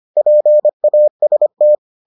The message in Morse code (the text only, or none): PAST